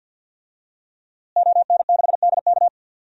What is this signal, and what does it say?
Morse code, heard as a keyed tone, about 51 wpm, 700 Hz. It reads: ON6DK
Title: DXCC-100 Call Signs - 5 Characters - Repeated 3x - 50wpm